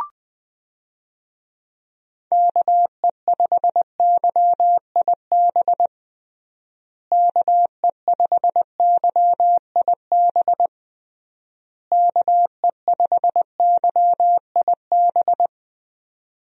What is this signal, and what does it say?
Morse code, heard as a keyed tone, about 21 wpm, 700 Hz. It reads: KE5YIB KE5YIB KE5YIB
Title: USA Call Signs - 6 Characters - Repeated 3x - 20wpm